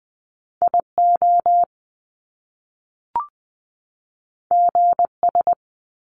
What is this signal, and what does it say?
Morse code, heard as a keyed tone, about 20 wpm, 700 Hz. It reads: IO GS
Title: Top Two Letter Combinations in Words 20wpm